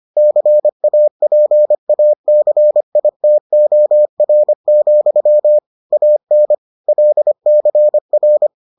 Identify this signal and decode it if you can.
Morse code, heard as a keyed tone, about 25 wpm, 600 Hz. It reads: CAPACITOR, AN LCR